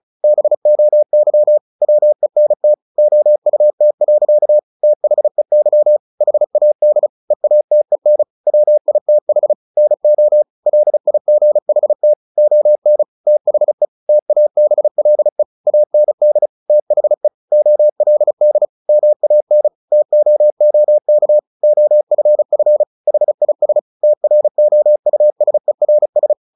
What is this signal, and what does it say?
Morse code, heard as a keyed tone, about 35 wpm, 600 Hz. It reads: BOY WENT OUT. THEY HAD EATEN WITH NO LIGHT ON THE TABLE AND THE OLD MAN TOOK OFF HIS TROUSERS
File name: the old man and the sea / the old man and the sea 1 35wpm